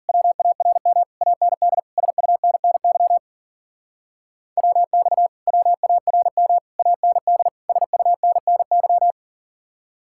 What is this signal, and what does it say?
Morse code, heard as a keyed tone, about 52 wpm, 700 Hz. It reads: WARM AND SUNNY WX WARM AND SUNNY